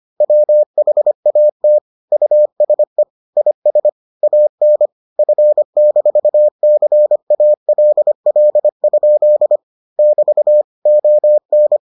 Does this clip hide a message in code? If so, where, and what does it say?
Yes: Morse code from 0.2 to 11.8 s, heard as a keyed tone, about 25 wpm, 600 Hz. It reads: WHAT USE IS AN F-CALL? <BT> ON